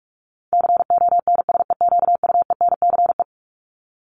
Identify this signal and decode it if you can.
Morse code, heard as a keyed tone, about 44 wpm, 700 Hz. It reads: CONSEQUENCE
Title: LY Suffix - Top 100 Words with Base Words 45wpm